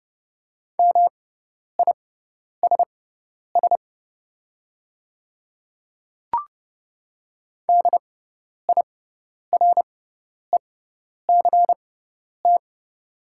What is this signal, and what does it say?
Morse code, heard as a keyed tone, about 31 wpm, spaced as for 10 wpm, 700 Hz. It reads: MISS DIRECT
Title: 6x Character Spacing - Top 500 Words - 30wpm